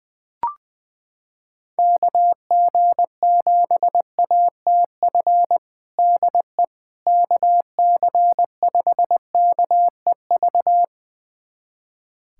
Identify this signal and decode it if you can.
Morse code, heard as a keyed tone, about 20 wpm, 700 Hz. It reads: KG7ATF DE KC5KEV
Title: USA Callsign DE Callsign - 6 Characters - 20wpm